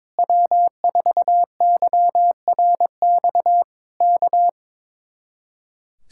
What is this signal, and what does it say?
Morse code, heard as a keyed tone, about 21 wpm, 700 Hz. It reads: W4YRX K
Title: Speed-Racing Common QSO Phrases - 22wpm